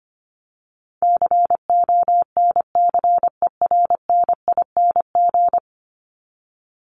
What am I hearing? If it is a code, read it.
Morse code, heard as a keyed tone, about 25 wpm, 700 Hz. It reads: CONCERNING
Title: ING Suffix - Top 100 Words with Base Words 25wpm